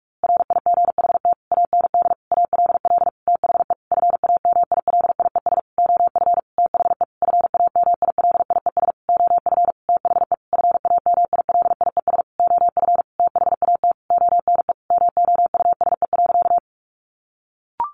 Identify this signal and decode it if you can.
Morse code, heard as a keyed tone, about 44 wpm, 700 Hz. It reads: RIGHT AND ALL THE FAMILIES OF THE FAMILIES OF THE FAMILIES OF THAT ONE MOUSE.
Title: A Sound Of Thunder 45wpm